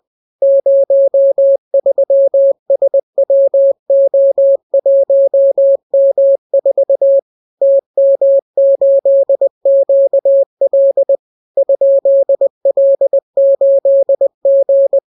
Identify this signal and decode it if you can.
Morse code, heard as a keyed tone, about 20 wpm, 550 Hz. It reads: 03SWO1M4 TM8QL ?L8G